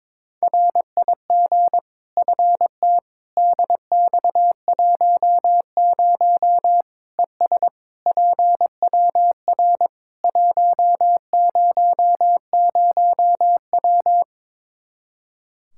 Morse code, heard as a keyed tone, about 22 wpm, 700 Hz. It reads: RIG FT DX10 ES PWR 100W
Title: K1USN SST QSO Practice 22wpm